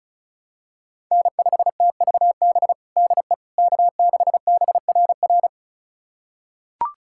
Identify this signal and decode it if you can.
Morse code, heard as a keyed tone, about 36 wpm, 700 Hz. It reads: N5TVB DE K6BRR
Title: USA Callsign DE Callsign - 5 Characters - 35wpm